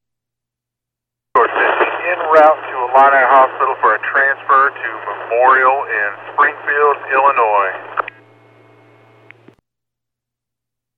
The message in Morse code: none

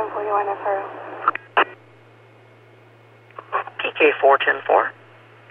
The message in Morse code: none

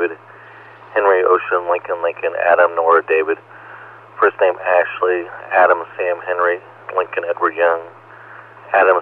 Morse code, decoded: none

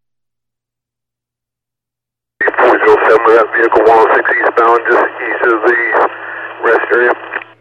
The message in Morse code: none

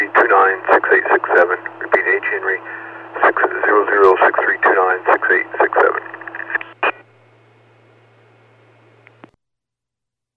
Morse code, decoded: none

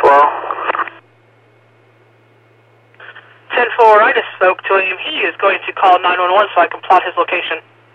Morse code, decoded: none

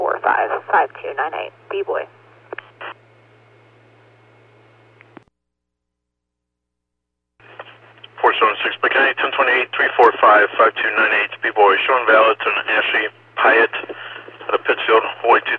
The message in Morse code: none

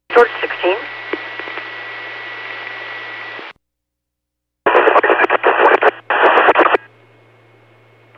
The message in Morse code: none

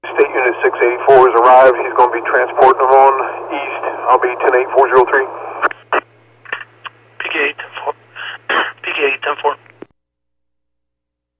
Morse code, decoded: none

